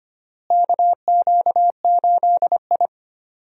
Morse code, heard as a keyed tone, about 25 wpm, 700 Hz. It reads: KQ8I
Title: USA Call Signs - 4 Characters - Repeated 3x - 25wpm